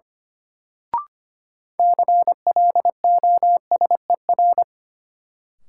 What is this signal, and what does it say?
Morse code, heard as a keyed tone, about 25 wpm, 700 Hz. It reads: CLOSER
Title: Sets of 5 Words Encore - New Word Drill - 25wpm